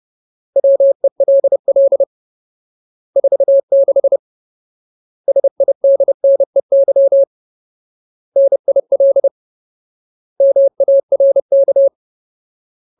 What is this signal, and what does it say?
Morse code, heard as a keyed tone, about 30 wpm, 550 Hz. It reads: WELL 46 SIDNEY NIL MARK
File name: Ultimate Word List / 30WPM Ultimate